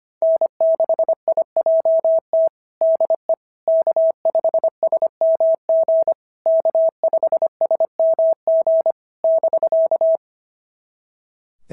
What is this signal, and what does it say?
Morse code, heard as a keyed tone, about 24 wpm, 650 Hz. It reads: N6IJT DE K5SMG K5SMG <BK>
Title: SOTA QSO Practice 25wpm